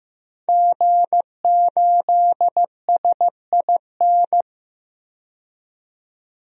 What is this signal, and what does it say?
Morse code, heard as a keyed tone, about 15 wpm, 700 Hz. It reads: G8SIN
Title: DXCC-100 Call Signs - 5 Characters - 15wpm